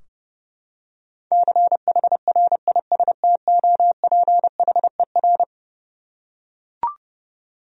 Morse code, heard as a keyed tone, about 30 wpm, 700 Hz. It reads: CHRISTOPHER
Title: USA Common Names - 30wpm